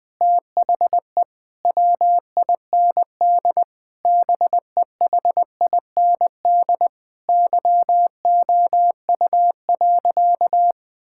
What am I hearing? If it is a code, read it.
Morse code, heard as a keyed tone, about 21 wpm, 700 Hz. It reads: THE WIND BEHIND YOU.
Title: Sentences from Top 1000 Words - Review Part 6 - 20wpm